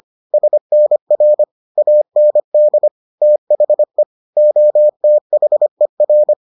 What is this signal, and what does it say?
Morse code, heard as a keyed tone, about 25 wpm, 600 Hz. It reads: SNR AND THE OTHER